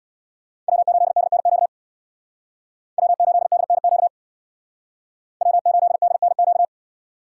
Morse code, heard as a keyed tone, about 62 wpm, 700 Hz. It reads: W8DNQ W8DNQ W8DNQ
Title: Speed Racing - USA Call Signs 50wpm